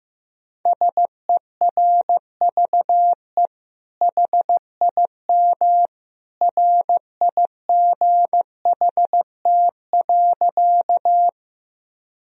Morse code, heard as a keyed tone, about 16 wpm, 700 Hz. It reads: SERVE HIM RIGHT.
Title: Sentences from Top 1000 Words - Review Part 2 - 15wpm